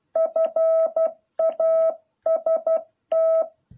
Morse code, heard as a keyed tone, about 12 wpm, 650 Hz. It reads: FAST
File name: jh8cbh / 2021cw